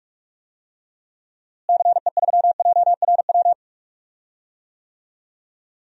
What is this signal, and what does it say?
Morse code, heard as a keyed tone, about 46 wpm, 700 Hz. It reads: CE3JRW
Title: DXCC-100 Call Signs - 6 Characters - Repeated 2x - 45wpm